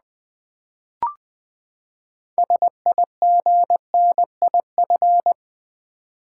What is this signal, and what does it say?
Morse code, heard as a keyed tone, about 20 wpm, 700 Hz. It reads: SIGNIF
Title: Build A Word - 6 Letters At A Time - 12-Letter Words - 20wpm